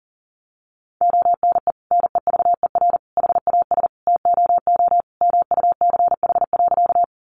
Morse code, heard as a keyed tone, about 40 wpm, 700 Hz. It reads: ONE NEVER HAS TOO MUCH.
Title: Sentences from Top 300 Words 40wpm